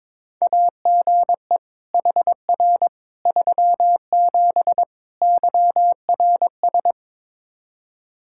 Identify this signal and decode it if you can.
Morse code, heard as a keyed tone, about 22 wpm, 700 Hz. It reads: AGE HR 37 YRS
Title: Practice QSO - Exchange 3 - More Info - 22wpm